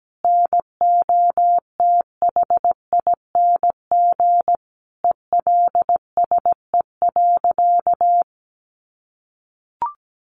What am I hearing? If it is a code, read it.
Morse code, heard as a keyed tone, about 17 wpm, 700 Hz. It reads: NOTHING ELSE.
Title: Sentences from Top 800 Words 17wpm